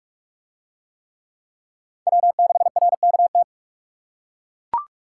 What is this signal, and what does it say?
Morse code, heard as a keyed tone, about 40 wpm, 700 Hz. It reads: W6RKT